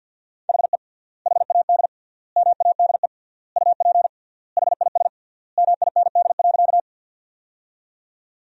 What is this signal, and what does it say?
Morse code, heard as a keyed tone, about 50 wpm, 700 Hz. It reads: HE HAD MADE UP HIS MIND.
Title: 3x Word-Spacing - Sentences from Top 500 Words - 50wpm